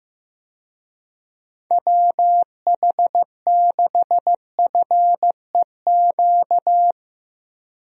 Morse code, heard as a keyed tone, about 15 wpm, 700 Hz. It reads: WH6FEQ
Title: USA Call Signs - 6 Characters - Repeated 2x - 15wpm